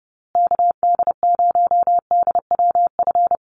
Morse code, heard as a keyed tone, about 30 wpm, 700 Hz. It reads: KD0DWF
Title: USA Call Signs Repeated 2x 30wpm